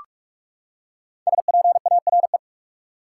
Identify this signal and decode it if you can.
Morse code, heard as a keyed tone, about 46 wpm, 700 Hz. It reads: SPARE
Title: Five Letter Words - 45wpm